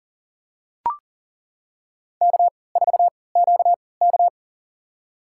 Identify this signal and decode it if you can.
Morse code, heard as a keyed tone, about 40 wpm, spaced as for 23 wpm, 700 Hz. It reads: K4QK
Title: USA Call Signs - 4 Characters - 3x Character Spacing - 40wpm